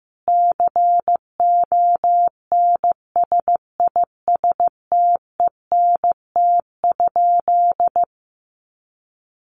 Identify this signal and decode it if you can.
Morse code, heard as a keyed tone, about 15 wpm, 700 Hz. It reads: CONSISTENT?
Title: Sentences from Top 1500 Words 15wpm